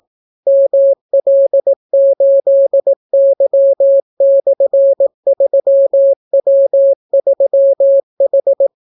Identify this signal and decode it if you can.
Morse code, heard as a keyed tone, about 18 wpm, 550 Hz. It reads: ML8Y/3W3H